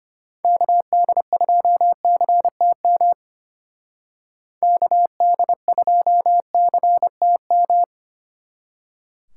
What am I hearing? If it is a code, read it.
Morse code, heard as a keyed tone, about 30 wpm, 700 Hz. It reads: KD2CTM KD2CTM
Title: Speed Racing - USA 2x3 Call Signs 25wpm